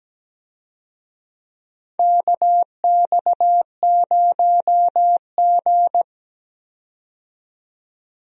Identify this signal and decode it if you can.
Morse code, heard as a keyed tone, about 17 wpm, 700 Hz. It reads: KX0G
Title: USA Call Signs - 4 Characters - 17wpm